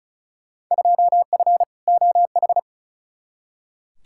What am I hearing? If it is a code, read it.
Morse code, heard as a keyed tone, about 36 wpm, 700 Hz. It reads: 2F OH
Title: Field Day Exchanges - 35wpm